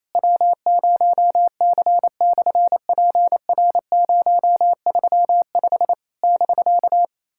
Morse code, heard as a keyed tone, about 28 wpm, 690 Hz. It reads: W0C/PR035 <BK>
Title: SOTA QSO Practice 28wpm